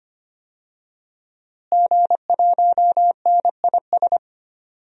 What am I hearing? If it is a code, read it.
Morse code, heard as a keyed tone, about 25 wpm, 700 Hz. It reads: G1NIS
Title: DXCC-100 Call Signs - 5 Characters - Repeated 3x - 25wpm